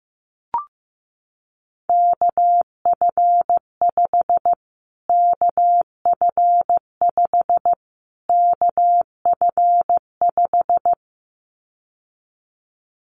Morse code, heard as a keyed tone, about 15 wpm, 700 Hz. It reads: KF5 KF5 KF5
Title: Callsign Prefixes for Top 25 DXCC - Repeated 3x - 15wpm